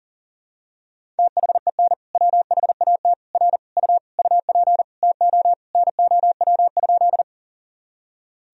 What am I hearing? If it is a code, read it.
Morse code, heard as a keyed tone, about 40 wpm, 700 Hz. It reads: THEN WHAT R U UP TO NOW?
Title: Sentences from Top 100 Words and Abbreviations 40wpm